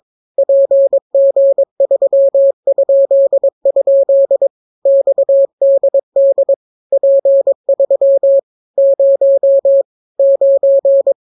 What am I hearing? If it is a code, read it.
Morse code, heard as a keyed tone, about 22 wpm, 550 Hz. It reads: PG3?? XDD P3 0 9